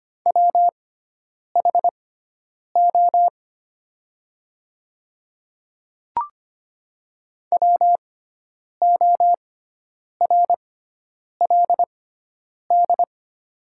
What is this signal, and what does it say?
Morse code, heard as a keyed tone, about 25 wpm, spaced as for 9 wpm, 700 Hz. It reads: WHO WORLD
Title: 6x Character Spacing - Top 200 Words - 25wpm